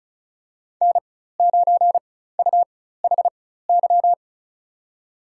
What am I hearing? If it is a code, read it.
Morse code, heard as a keyed tone, about 36 wpm, spaced as for 16 wpm, 700 Hz. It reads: N9UHY
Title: USA Call Signs - 5 Characters - 4x Character Spacing - 35wpm